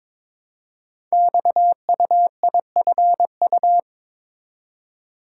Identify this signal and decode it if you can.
Morse code, heard as a keyed tone, about 23 wpm, 700 Hz. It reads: XUIFU
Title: Code Groups - 5 Letters - 22wpm